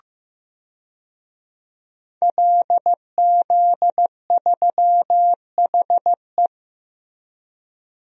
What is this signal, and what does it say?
Morse code, heard as a keyed tone, about 15 wpm, 700 Hz. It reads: LZ3HE
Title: DXCC-100 Call Signs - 5 Characters - Repeated 2x - 15wpm